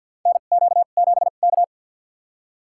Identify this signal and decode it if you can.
Morse code, heard as a keyed tone, about 53 wpm, spaced as for 37 wpm, 690 Hz. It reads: NQ7K
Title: USA Call Signs - 2x Character Spacing - 50wpm